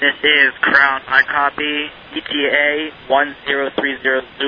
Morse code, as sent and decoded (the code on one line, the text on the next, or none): none